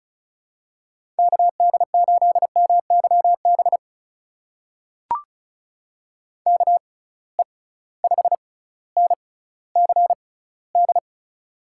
Morse code, heard as a keyed tone, about 36 wpm, spaced as for 18 wpm, 700 Hz. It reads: KD8MYB KE5NCD
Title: USA Call Signs - 6 Characters - 6x Character Spacing - 35wpm